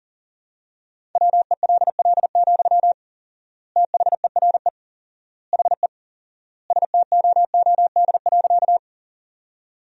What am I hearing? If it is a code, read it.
Morse code, heard as a keyed tone, about 40 wpm, 700 Hz. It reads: WELL, THERE HE STOOD.
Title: 4x Word-Spacing - Sentences from Top 500 Words - 40wpm